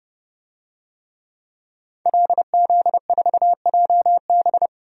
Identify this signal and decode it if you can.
Morse code, heard as a keyed tone, about 31 wpm, 700 Hz. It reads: LZ4JB
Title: DXCC-100 Call Signs - 5 Characters - 30wpm